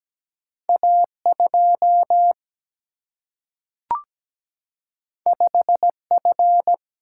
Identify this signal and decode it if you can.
Morse code, heard as a keyed tone, about 17 wpm, 700 Hz. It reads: A2 5F
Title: Single-Letter Single-Number 17wpm